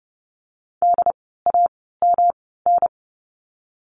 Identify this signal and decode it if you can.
Morse code, heard as a keyed tone, about 30 wpm, spaced as for 17 wpm, 700 Hz. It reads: DAMN
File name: course / 30:10wpm / 108 Lesson 18f - Letter m - Words 30wpm